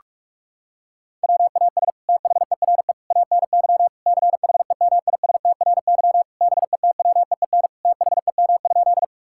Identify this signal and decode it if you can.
Morse code, heard as a keyed tone, about 46 wpm, 690 Hz. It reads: WAS THERE ANY CHEMISTRY BETWEEN THEM?
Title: Sentences from Top 1000 Words - Review Part 6 - 45wpm